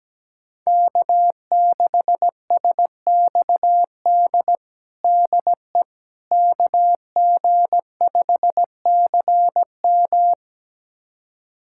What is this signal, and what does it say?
Morse code, heard as a keyed tone, about 17 wpm, 700 Hz. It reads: K6SXD DE KG5CM
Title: USA Callsign DE Callsign - 5 Characters - 17wpm